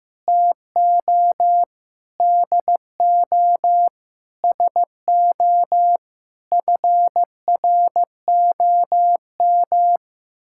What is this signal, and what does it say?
Morse code, heard as a keyed tone, about 15 wpm, 700 Hz. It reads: TO DO SO FROM
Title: Sentences from Top 1600 Words 15wpm